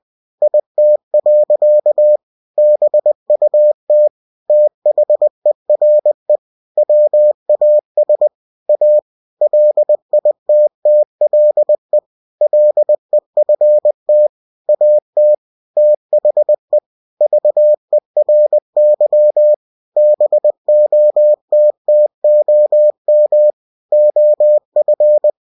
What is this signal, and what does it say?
Morse code, heard as a keyed tone, about 20 wpm, 600 Hz. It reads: IT. BUT THERE WAS A LITTLE LEFT AT THE VERY BOTTOM OF
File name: cw-books / winnie-the-pooh / winnie-20wpm-page-0040